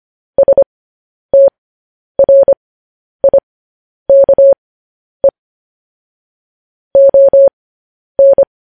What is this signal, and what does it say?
Morse code, heard as a keyed tone, about 25 wpm, spaced as for 10 wpm, 550 Hz. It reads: STRIKE ON
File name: podcast / News headlines-1x-25x10